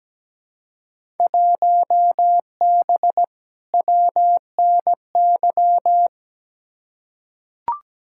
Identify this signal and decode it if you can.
Morse code, heard as a keyed tone, about 17 wpm, 700 Hz. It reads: 1B WNY
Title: Field Day Exchanges Encore - 17wpm